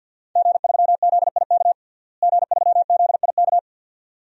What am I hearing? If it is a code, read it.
Morse code, heard as a keyed tone, about 48 wpm, 700 Hz. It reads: G3ZIK G3ZIK
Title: DXCC-100 Call Signs - 5 Characters - Repeated 2x - 50wpm